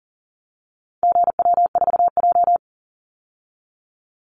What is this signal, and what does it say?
Morse code, heard as a keyed tone, about 40 wpm, 700 Hz. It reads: GW4J
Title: Call Signs Repeated 2x - 40wpm